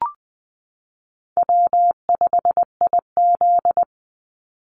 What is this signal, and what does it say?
Morse code, heard as a keyed tone, about 20 wpm, 700 Hz. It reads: W5IZ
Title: Easy Call Signs Repeated 3x - 20wpm